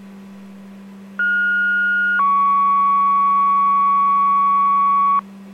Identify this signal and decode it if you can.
Two-tone page, heard as a one-second tone, then a three-second tone, about 1430 then 1120 Hz. It is heard from 1.2 to 5.2 s.